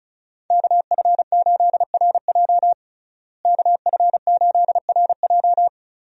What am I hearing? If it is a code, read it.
Morse code, heard as a keyed tone, about 34 wpm, 700 Hz. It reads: KF8RJ KF8RJ
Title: USA Call Signs - 5 Characters - Repeated 3x - 35wpm